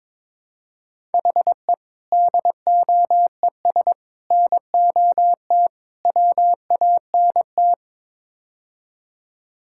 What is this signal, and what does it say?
Morse code, heard as a keyed tone, about 21 wpm, 700 Hz. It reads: HE DOES NOT WANT
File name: practice / 22wpm / Build A Phrase - 5-Word Phrases - 22wpm